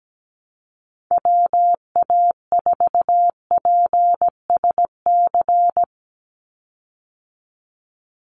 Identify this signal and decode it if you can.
Morse code, heard as a keyed tone, about 17 wpm, 700 Hz. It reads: WA4PSC
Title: Call Signs Repeated 2x - 17wpm